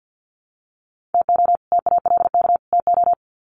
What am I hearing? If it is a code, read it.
Morse code, heard as a keyed tone, about 50 wpm, 700 Hz. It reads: TO TALK TO
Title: Sets of 3 Words 50wpm